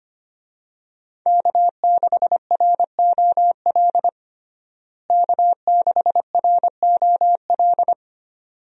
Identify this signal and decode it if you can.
Morse code, heard as a keyed tone, about 25 wpm, 700 Hz. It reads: K6ROL K6ROL
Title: USA Call Signs - 5 Characters - Repeated 2x - 25wpm